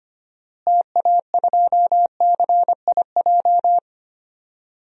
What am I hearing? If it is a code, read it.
Morse code, heard as a keyed tone, about 25 wpm, 700 Hz. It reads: TA2CIJ
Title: DXCC-100 Call Signs - 6 Characters - Repeated 3x - 25wpm